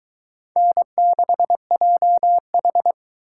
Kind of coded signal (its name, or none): Morse code